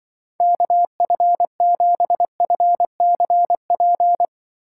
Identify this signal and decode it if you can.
Morse code, heard as a keyed tone, about 25 wpm, 700 Hz. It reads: KF7FCP
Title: Speed Racing - USA Call Signs 20wpm